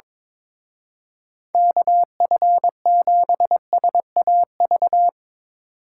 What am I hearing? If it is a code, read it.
Morse code, heard as a keyed tone, about 22 wpm, 700 Hz. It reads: KF7SAV
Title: Speed Racing - USA 2x3 Call Signs 22wpm